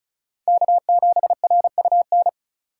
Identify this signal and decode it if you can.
Morse code, heard as a keyed tone, about 36 wpm, 700 Hz. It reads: K7RUN
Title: Speed Racing - USA 1x3 Call Signs 30wpm